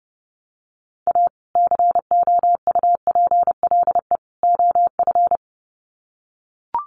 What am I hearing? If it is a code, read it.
Morse code, heard as a keyed tone, about 29 wpm, 700 Hz. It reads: A COUPLE OF
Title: Sets of 3 Words 30wpm